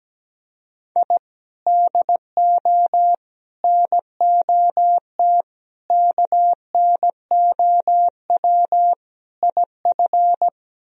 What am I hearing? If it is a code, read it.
Morse code, heard as a keyed tone, about 17 wpm, 700 Hz. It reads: I DO NOT KNOW IF
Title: Speed-Racing - Sets of 5 Words 17wpm